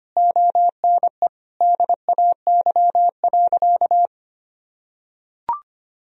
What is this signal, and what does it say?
Morse code, heard as a keyed tone, about 25 wpm, 700 Hz. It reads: ONE DAY.
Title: Sentences from Top 1000 Words - Review Part 1 - 25wpm